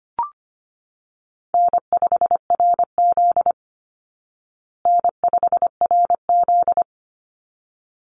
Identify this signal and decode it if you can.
Morse code, heard as a keyed tone, about 25 wpm, 700 Hz. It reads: N5RZ N5RZ
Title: Easy Call Signs Repeated 2x - 25wpm